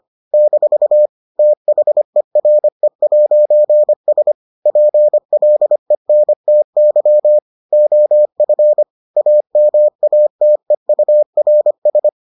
Morse code, heard as a keyed tone, about 25 wpm, 600 Hz. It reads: - THERE'S PLENTY OF AMATEURS